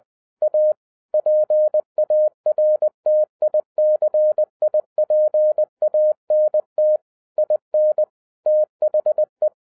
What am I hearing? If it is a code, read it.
Morse code, heard as a keyed tone, about 20 wpm, 610 Hz. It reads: A PARTICIPANT IN THE
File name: 2012-01-mp3 / n16-THROUGH EIGHT MONTHS-20wpm